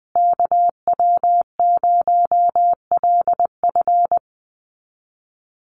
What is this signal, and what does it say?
Morse code, heard as a keyed tone, about 20 wpm, 700 Hz. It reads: KW0LF
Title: USA Call Signs Repeated 3x 20wpm